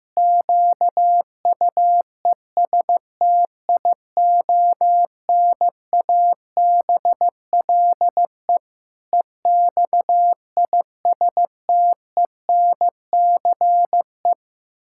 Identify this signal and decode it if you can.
Morse code, heard as a keyed tone, about 15 wpm, 700 Hz. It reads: QUESTIONABLE EXISTENCE
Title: Pangram Challenge v2 - 15wpm